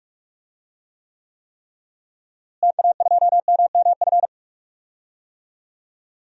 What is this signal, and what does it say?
Morse code, heard as a keyed tone, about 43 wpm, 700 Hz. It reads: TA2MMF
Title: DXCC-100 Call Signs - 6 Characters - Repeated 3x - 45wpm